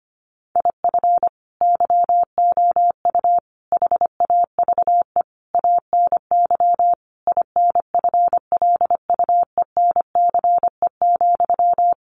Morse code, heard as a keyed tone, about 25 wpm, 700 Hz. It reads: IF YOU HAVE ANY INFLUENCE,
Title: Sentences from Top 1100 Words 25wpm